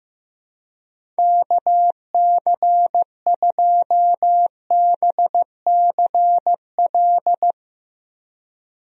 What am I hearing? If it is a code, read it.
Morse code, heard as a keyed tone, about 15 wpm, 700 Hz. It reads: KC2BCL
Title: USA Call Signs - 6 Characters - Repeated 2x - 15wpm